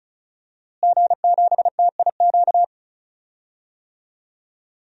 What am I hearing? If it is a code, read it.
Morse code, heard as a keyed tone, about 34 wpm, 700 Hz. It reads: G7TIQ
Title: DXCC-100 Call Signs - 5 Characters - Repeated 3x - 35wpm